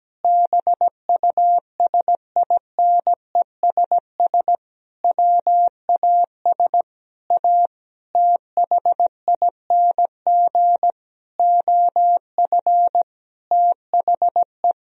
Morse code, heard as a keyed tone, about 17 wpm, 700 Hz. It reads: BUSINESS WAS A THING OF THE
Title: Sentences from Top 1000 Words - Review Part 3 - 17wpm